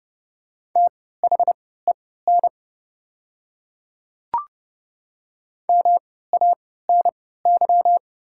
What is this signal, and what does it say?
Morse code, heard as a keyed tone, about 30 wpm, spaced as for 17 wpm, 700 Hz. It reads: THEN MANY
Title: 3x Character Spacing - Top 100 Words - 30wpm